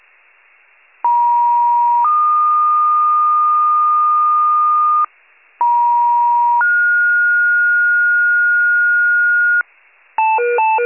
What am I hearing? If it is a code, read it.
Two-tone page, heard as a one-second tone, then a three-second tone, about 950 then 1220 Hz, about 950 then 1430 Hz, with noise. It is heard from 1.0 to 9.6 s.